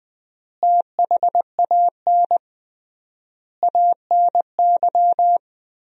Morse code, heard as a keyed tone, about 20 wpm, 700 Hz. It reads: THAN ANY
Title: 3x Word-Spacing - Sentences from Top 400 Words - 20wpm